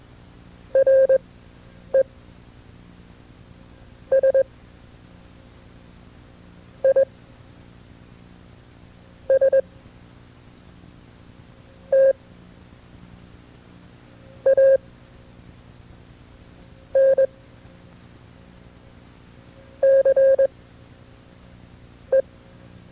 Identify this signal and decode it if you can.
Morse code, heard as a keyed tone, about 21 wpm, spaced as for 4 wpm, 530 Hz. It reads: RESISTANCE
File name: FreeFiles / CW-practice / KB6MT-tape2-side2